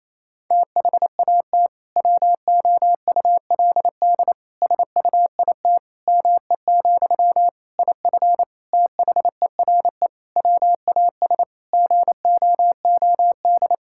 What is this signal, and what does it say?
Morse code, heard as a keyed tone, about 27 wpm, 700 Hz. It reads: THAT WOULD SUIT ME, IF THERE WAS GOOD
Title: Sentences from Top 1600 Words 28wpm